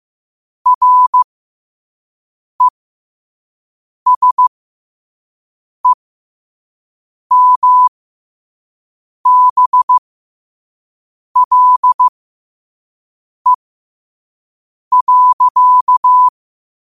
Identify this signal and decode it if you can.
Morse code, heard as a keyed tone, about 15 wpm, spaced as for 5 wpm, 1000 Hz. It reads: RESEMBLE.
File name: qotdfiles / 5wpm